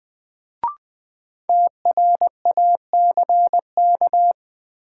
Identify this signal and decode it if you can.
Morse code, heard as a keyed tone, about 20 wpm, 700 Hz. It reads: TRACK
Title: Sentences from Top 1100 Words - New Word Drill - 20wpm